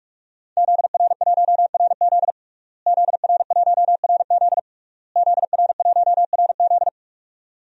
Morse code, heard as a keyed tone, about 46 wpm, 700 Hz. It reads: ZR1RZ ZR1RZ ZR1RZ
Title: DXCC-100 Call Signs - 5 Characters - Repeated 3x - 45wpm